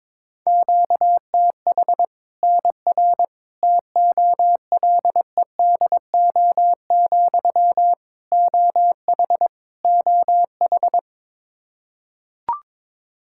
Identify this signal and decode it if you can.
Morse code, heard as a keyed tone, about 21 wpm, 700 Hz. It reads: QTH NR TOLEDO, OH OH
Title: Practice QSO - Exchange 1 - Beginning - 22wpm